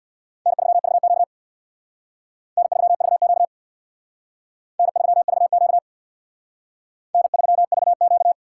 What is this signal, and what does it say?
Morse code, heard as a keyed tone, about 62 wpm, 700 Hz. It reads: N3VQ N3VQ N3VQ N3VQ
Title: Speed Racing - USA 1x2 Call Signs 50wpm